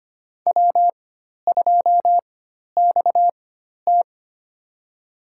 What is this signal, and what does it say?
Morse code, heard as a keyed tone, about 25 wpm, spaced as for 12 wpm, 700 Hz. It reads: W2XT
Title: USA Call Signs - 4 Characters - 4x Character Spacing - 25wpm